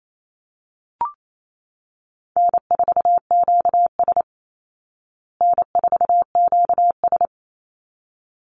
Morse code, heard as a keyed tone, about 28 wpm, 700 Hz. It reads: N4QS N4QS
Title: Easy Call Signs Repeated 2x - 28wpm